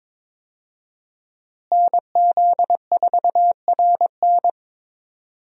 Morse code, heard as a keyed tone, about 22 wpm, 700 Hz. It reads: NZ4RN
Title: USA Call Signs - 5 Characters - Repeated 2x - 22wpm